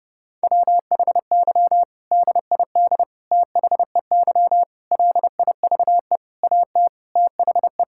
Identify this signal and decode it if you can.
Morse code, heard as a keyed tone, about 30 wpm, 700 Hz. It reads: WHY DID THEY LIVE AT THE
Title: Sentences from Top 1000 Words - Review Part 6 - 30wpm